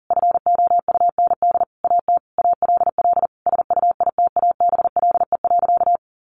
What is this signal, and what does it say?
Morse code, heard as a keyed tone, about 40 wpm, 700 Hz. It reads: FOUND AT ALL SUITABLE.
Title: Sentences from Top 1400 Words 40wpm